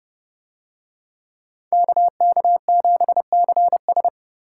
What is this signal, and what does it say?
Morse code, heard as a keyed tone, about 30 wpm, 700 Hz. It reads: KK7CS